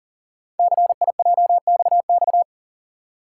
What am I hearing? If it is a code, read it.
Morse code, heard as a keyed tone, about 38 wpm, 690 Hz. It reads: CIJXX
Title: Code Groups - 5 Letters - 40wpm